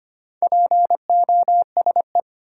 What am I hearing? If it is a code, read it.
Morse code, heard as a keyed tone, about 25 wpm, 700 Hz. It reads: POSE
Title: Sentences from Top 1000 Words - Review Part 2 - New Word Drill - 25wpm